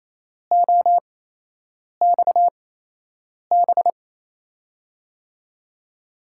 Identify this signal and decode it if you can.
Morse code, heard as a keyed tone, about 28 wpm, spaced as for 8 wpm, 700 Hz. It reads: OXB
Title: USA Call Signs - 6 Characters - 8x Character Spacing - 28wpm